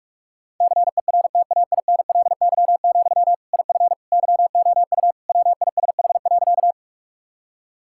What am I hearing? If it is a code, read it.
Morse code, heard as a keyed tone, about 45 wpm, 700 Hz. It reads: CERTAINLY, IF YOU WISH.
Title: Sentences from Top 1600 Words 45wpm